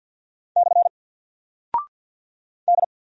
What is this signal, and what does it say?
Morse code, heard as a keyed tone, about 48 wpm, 700 Hz. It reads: / D
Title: Single Character (Letter:Number:Punctuation:Prosign) 50wpm